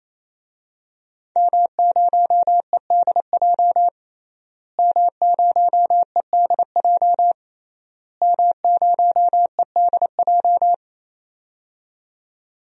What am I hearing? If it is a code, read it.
Morse code, heard as a keyed tone, about 28 wpm, 700 Hz. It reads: M0EDJ M0EDJ M0EDJ
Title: DXCC-100 Call Signs - 5 Characters - Repeated 3x - 28wpm